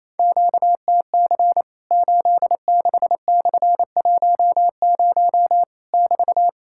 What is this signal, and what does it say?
Morse code, heard as a keyed tone, about 28 wpm, 700 Hz. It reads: QTC 86/10 <BT>
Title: Worked All Europe DX Contest - QTC Exchange - 28wpm